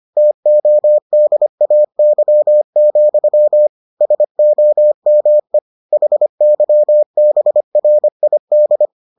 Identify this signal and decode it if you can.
Morse code, heard as a keyed tone, about 25 wpm, 600 Hz. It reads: TODAY, SOME HYBRID